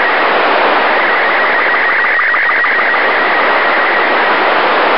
SSTV (slow-scan television), Robot 36.
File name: luxorion / Radio / suitsat-fm-sstv-17feb2006-0143z